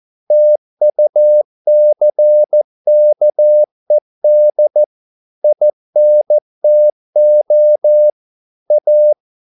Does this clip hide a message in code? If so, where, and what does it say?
Yes: Morse code from 0.3 to 9.1 s, heard as a keyed tone, about 14 wpm, 600 Hz. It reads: TUCKED INTO A